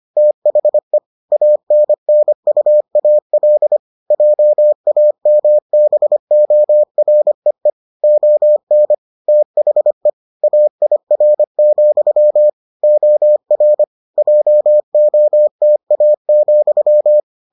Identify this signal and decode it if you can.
Morse code, heard as a keyed tone, about 25 wpm, 600 Hz. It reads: THE ANNUAL JAMBOREE ON THE AIR, OR JOTA,